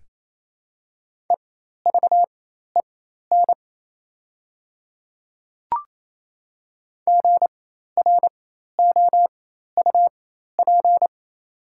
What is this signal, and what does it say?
Morse code, heard as a keyed tone, about 28 wpm, spaced as for 13 wpm, 700 Hz. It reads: EVEN GROUP
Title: 4x Character Spacing - Top 500 Words - 28wpm